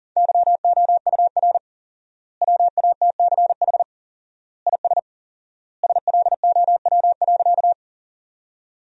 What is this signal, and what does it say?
Morse code, heard as a keyed tone, about 37 wpm, 700 Hz. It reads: YOUR WATCH IS SLOW.
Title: 4x Word-Spacing - Sentences from Top 400 Words - 40wpm